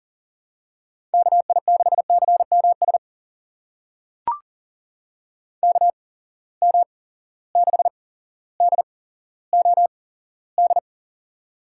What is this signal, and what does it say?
Morse code, heard as a keyed tone, about 40 wpm, spaced as for 17 wpm, 700 Hz. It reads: KI6CMS KM6DOD
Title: USA Call Signs - 6 Characters - 8x Character Spacing - 40wpm